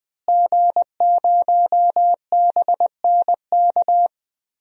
Morse code, heard as a keyed tone, about 20 wpm, 700 Hz. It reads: G0BNK
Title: DXCC-100 Call Signs - 5 Characters - Repeated 2x - 20wpm